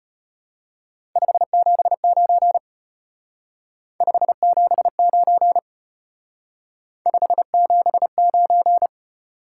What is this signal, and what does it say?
Morse code, heard as a keyed tone, about 30 wpm, 700 Hz. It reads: 579 579 579
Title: Speed-Racing Top 180 QSO Elements 25wpm